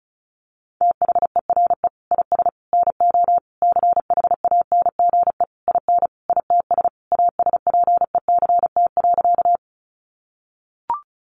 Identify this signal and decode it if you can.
Morse code, heard as a keyed tone, about 34 wpm, 700 Hz. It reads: THERE IS NO CHANGE IN ITS ASPECT.
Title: Sentences from Top 1300 Words 35wpm